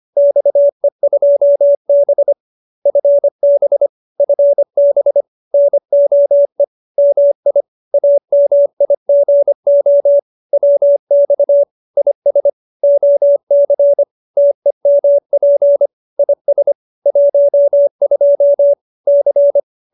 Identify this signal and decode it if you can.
Morse code, heard as a keyed tone, about 25 wpm, 570 Hz. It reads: XE2B FB FB NOE MI AMIGO WX IS OC TEMP IS 12 C
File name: QSO304 25